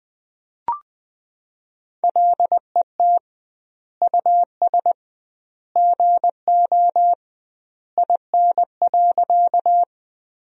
Morse code, heard as a keyed tone, about 21 wpm, 700 Hz. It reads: LET US GO IN.
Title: 2x Word-Spacing - Sentences from Top 300 Words - 20wpm